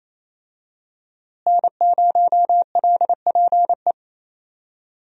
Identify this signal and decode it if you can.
Morse code, heard as a keyed tone, about 28 wpm, 700 Hz. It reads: N0LPE
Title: USA Call Signs - 5 Characters - Repeated 2x - 28wpm